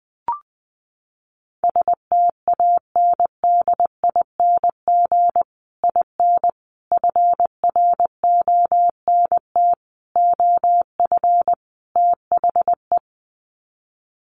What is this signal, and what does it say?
Morse code, heard as a keyed tone, about 20 wpm, 700 Hz. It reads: STANDING IN FRONT OF THE
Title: Sets of 5 Words Encore2 20wpm